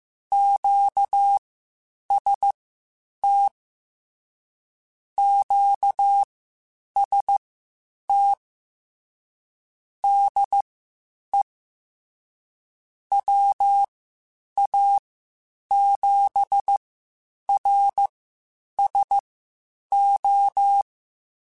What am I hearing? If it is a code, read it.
Morse code, heard as a keyed tone, about 15 wpm, spaced as for 8 wpm, 800 Hz. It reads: QST QST DE WA7RSO